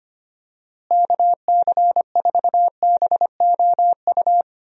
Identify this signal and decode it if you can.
Morse code, heard as a keyed tone, about 25 wpm, 700 Hz. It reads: KC4BOU
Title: USA Call Signs - 6 Characters - Repeated 2x - 25wpm